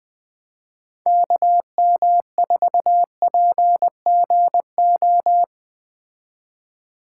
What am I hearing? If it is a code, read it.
Morse code, heard as a keyed tone, about 20 wpm, 700 Hz. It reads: KM4PGO
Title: Speed Racing - USA 2x3 Call Signs 17wpm